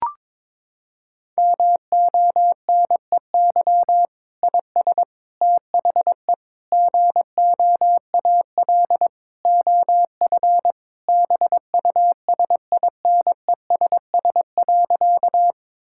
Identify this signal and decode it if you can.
Morse code, heard as a keyed tone, about 22 wpm, 700 Hz. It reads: MONEY IS THE GOAL OF BUSINESS.
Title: Sentences from Top 1000 Words - Review Part 5 - 22wpm